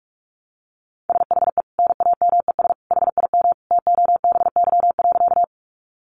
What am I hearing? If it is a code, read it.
Morse code, heard as a keyed tone, about 44 wpm, 700 Hz. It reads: SHE NAMES HIM TOBY.
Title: Easy Phrases 45wpm